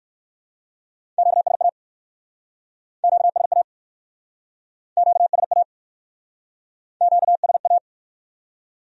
Morse code, heard as a keyed tone, about 57 wpm, 700 Hz. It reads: QSA QSA QSA QSA